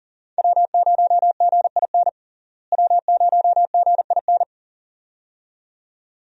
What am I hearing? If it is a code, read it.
Morse code, heard as a keyed tone, about 39 wpm, 700 Hz. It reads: W0GIN W0GIN
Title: USA Call Signs - 5 Characters - Repeated 2x - 40wpm